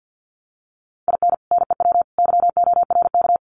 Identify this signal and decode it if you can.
Morse code, heard as a keyed tone, about 50 wpm, 700 Hz. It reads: IN NEW YORK
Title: Sets of 3 Words 50wpm